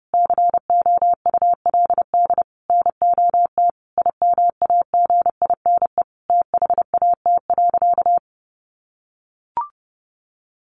Morse code, heard as a keyed tone, about 30 wpm, 700 Hz. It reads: COULD NOT IMAGINE THAT.
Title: Sentences from Top 1400 Words 30wpm